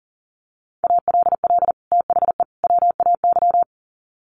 Morse code, heard as a keyed tone, about 40 wpm, 700 Hz. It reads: ALL THE WAY
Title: Sets of 3 Words 40wpm